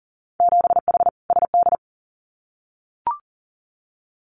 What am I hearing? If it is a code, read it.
Morse code, heard as a keyed tone, about 40 wpm, 700 Hz. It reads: 7H SD
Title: Winter Field Day Exchanges Encore 40wpm